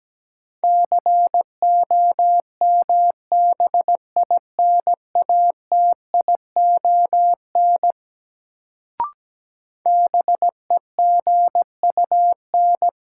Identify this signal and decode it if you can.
Morse code, heard as a keyed tone, about 17 wpm, 700 Hz. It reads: COMBINATION BEGUN
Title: Sentences from Top 1000 Words - Review Part 6 - New Word Drill - 17wpm